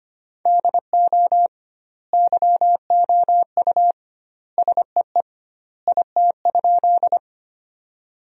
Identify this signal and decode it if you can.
Morse code, heard as a keyed tone, about 25 wpm, 700 Hz. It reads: DO YOU SEE IT?
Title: 2x Word-Spacing - Sentences from Top 100 Words - 25wpm